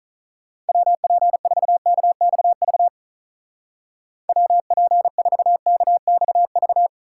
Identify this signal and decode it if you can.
Morse code, heard as a keyed tone, about 44 wpm, 700 Hz. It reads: WP4KXV WP4KXV